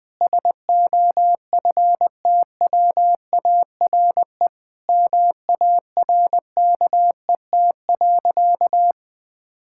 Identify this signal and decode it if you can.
Morse code, heard as a keyed tone, about 20 wpm, 700 Hz. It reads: SOFTWARE MARKET.
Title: Sentences from Top 1000 Words - Review Part 6 - 20wpm